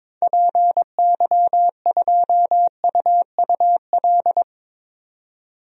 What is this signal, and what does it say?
Morse code, heard as a keyed tone, about 22 wpm, 700 Hz. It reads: PY2UUL
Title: DXCC-100 Call Signs - 6 Characters - 22wpm